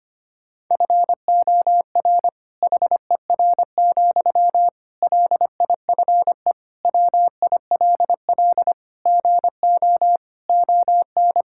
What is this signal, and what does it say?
Morse code, heard as a keyed tone, about 25 wpm, 700 Hz. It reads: FOR HER, LIFE WILL GO ON
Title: Sentences from Top 1600 Words 25wpm